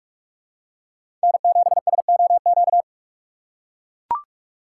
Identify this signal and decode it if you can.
Morse code, heard as a keyed tone, about 41 wpm, 690 Hz. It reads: N7SOQ
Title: USA Call Signs - 5 Characters - 2x Character Spacing - 45wpm